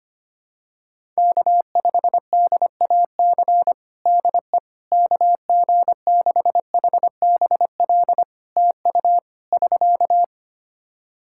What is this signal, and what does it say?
Morse code, heard as a keyed tone, about 25 wpm, 700 Hz. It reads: K5DAC DE KG6HBL TU <SK>
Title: K1USN SST QSO Practice 25wpm